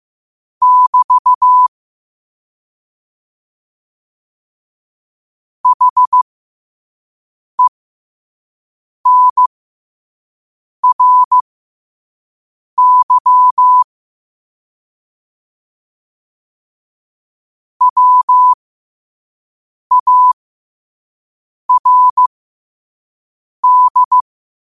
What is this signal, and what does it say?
Morse code, heard as a keyed tone, about 15 wpm, spaced as for 5 wpm, 1000 Hz. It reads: <BT> HENRY WARD